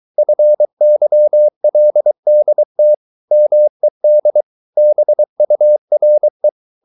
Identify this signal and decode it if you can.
Morse code, heard as a keyed tone, about 23 wpm, 600 Hz. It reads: FYLDT MED BURE
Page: Ferie 23 wpm